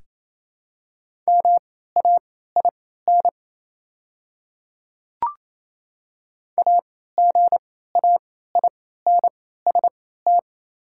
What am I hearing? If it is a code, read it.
Morse code, heard as a keyed tone, about 28 wpm, spaced as for 16 wpm, 690 Hz. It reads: MAIN AGAINST
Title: 3x Character Spacing - Top 500 Words - 28wpm